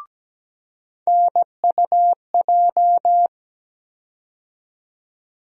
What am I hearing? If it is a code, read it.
Morse code, heard as a keyed tone, about 17 wpm, 700 Hz. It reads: NUJ
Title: USA Call Sign Postfixes - 3 Letters 17wpm